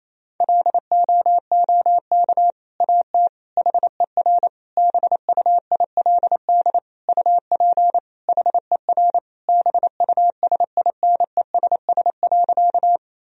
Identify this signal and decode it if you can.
Morse code, heard as a keyed tone, about 28 wpm, 700 Hz. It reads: LOOK AT HER BUILD UP HER BUSINESS.
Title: Sentences from Top 1000 Words - Review Part 1 - 28wpm